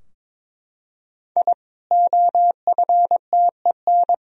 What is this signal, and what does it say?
Morse code, heard as a keyed tone, about 22 wpm, 700 Hz. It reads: I OFTEN
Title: Sentences from Top 1000 Words - Review Part 4 - 22wpm